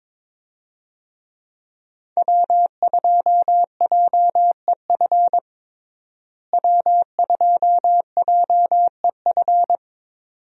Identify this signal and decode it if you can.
Morse code, heard as a keyed tone, about 22 wpm, 700 Hz. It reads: W2JEF W2JEF
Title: USA Call Signs - 5 Characters - Repeated 2x - 22wpm